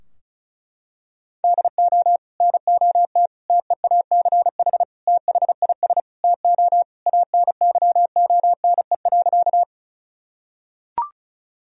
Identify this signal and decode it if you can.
Morse code, heard as a keyed tone, about 36 wpm, 700 Hz. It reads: DO NOT TEACH THIS TO ANYONE.
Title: Sentences from Top 1000 Words - Review Part 2 - 35wpm